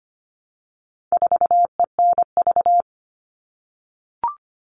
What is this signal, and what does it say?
Morse code, heard as a keyed tone, about 25 wpm, 700 Hz. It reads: 4ENV
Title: All Cut Numbers - 4-Digit 25wpm